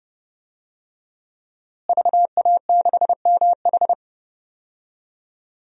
Morse code, heard as a keyed tone, about 32 wpm, 700 Hz. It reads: VA6MH
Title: DXCC-100 Call Signs - 5 Characters - Repeated 3x - 30wpm